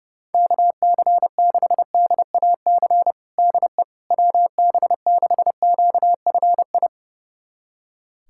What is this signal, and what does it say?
Morse code, heard as a keyed tone, about 30 wpm, 700 Hz. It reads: KC6DAC DE WB6QFI
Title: USA Callsign DE Callsign - 6 Characters - 30wpm